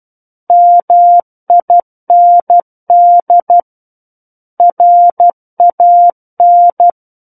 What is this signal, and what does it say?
Morse code, heard as a keyed tone, about 12 wpm, 700 Hz. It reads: MIND RAN
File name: Book One / 03-Chapter 3 - 12 WPM